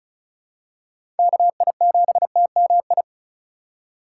Morse code, heard as a keyed tone, about 36 wpm, 700 Hz. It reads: KI7TMI
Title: USA Call Signs - 4x Character Spacing - 35wpm